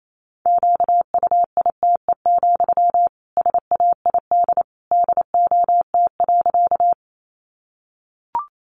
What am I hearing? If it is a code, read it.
Morse code, heard as a keyed tone, about 28 wpm, 700 Hz. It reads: QUITE, SAID DOT.
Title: Sentences from Top 1500 Words 28wpm